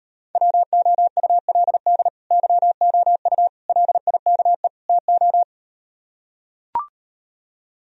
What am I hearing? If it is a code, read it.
Morse code, heard as a keyed tone, about 38 wpm, 700 Hz. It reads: WOULD YOU LIKE TO